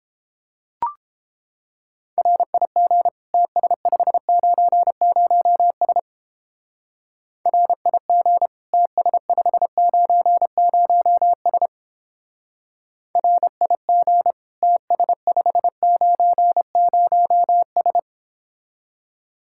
Morse code, heard as a keyed tone, about 29 wpm, 700 Hz. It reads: RIG TS590S RIG TS590S RIG TS590S